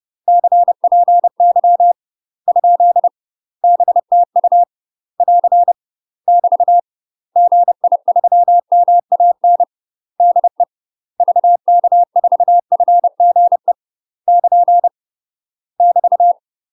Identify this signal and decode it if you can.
Morse code, heard as a keyed tone, about 30 wpm, 700 Hz. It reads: CPY ? BTU <AR> <BT> GI3MAN DE VK4FGE <KN> <BT>
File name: UK MASTER QSOs TRAINING / 30WPM / UK MASTER QSOs 0001